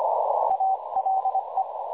Morse code, heard as a keyed tone, about 23 wpm, 770 Hz, with noise. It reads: ALE
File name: dlazaawansowanych / nagraniazpasma / websdr recording 3565.0kHz